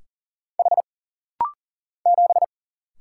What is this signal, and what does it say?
Morse code, heard as a keyed tone, about 41 wpm, 700 Hz. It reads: H 7
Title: Single Letter-Number Alt - Rapid-Fire 40wpm